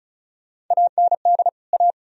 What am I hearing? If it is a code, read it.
Morse code, heard as a keyed tone, about 48 wpm, spaced as for 42 wpm, 690 Hz. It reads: AND A
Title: Speed-Racing - Sets of 2 Words 35wpm